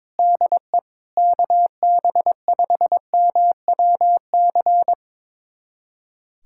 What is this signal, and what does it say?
Morse code, heard as a keyed tone, about 22 wpm, 700 Hz. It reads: DE KB5MWC
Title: USA Callsign DE Callsign - 6 Characters - 22wpm